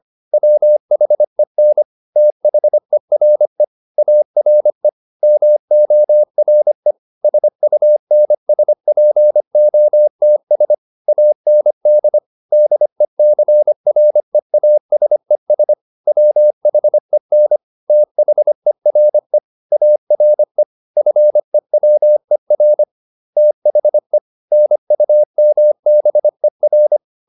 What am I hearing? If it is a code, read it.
Morse code, heard as a keyed tone, about 25 wpm, 600 Hz. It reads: WHEN THERE ARE MORE SUNSPOTS AND DECREASES WHEN THERE ARE FEWER THE NUMBER